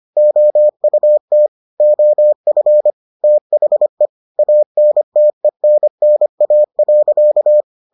Morse code, heard as a keyed tone, about 25 wpm, 600 Hz. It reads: OUT OF THE ANTENNA.